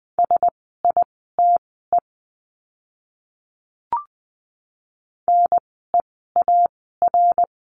Morse code, heard as a keyed tone, about 20 wpm, spaced as for 15 wpm, 700 Hz. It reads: SITE NEAR
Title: 054 Lesson 10f - Letter r - Words 20wpm